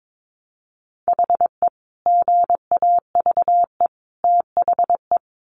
Morse code, heard as a keyed tone, about 22 wpm, 700 Hz. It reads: HE GAVE THE
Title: Sentences from Top 1300 Words 22wpm